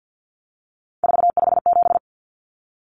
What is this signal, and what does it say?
Morse code, heard as a keyed tone, about 48 wpm, 700 Hz. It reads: 457
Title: Numbers - 3-Digit 50wpm